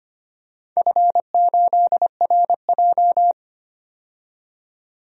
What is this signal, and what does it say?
Morse code, heard as a keyed tone, about 25 wpm, 700 Hz. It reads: F8RJ
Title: DXCC-100 Call Signs - 4 Characters - 25wpm